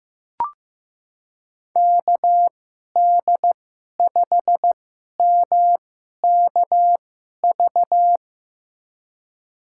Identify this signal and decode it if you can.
Morse code, heard as a keyed tone, about 15 wpm, spaced as for 11 wpm, 700 Hz. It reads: KD5MKV
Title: USA Call Signs - 6 Characters - 2x Character Spacing - 15wpm